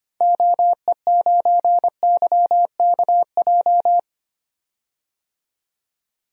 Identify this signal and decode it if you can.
Morse code, heard as a keyed tone, about 25 wpm, 700 Hz. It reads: OE9YKJ
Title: DXCC-100 Call Signs - 6 Characters - 25wpm